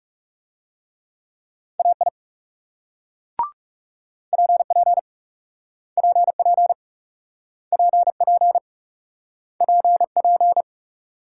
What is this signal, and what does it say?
Morse code, heard as a keyed tone, about 38 wpm, 700 Hz. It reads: AI PP PP PP PP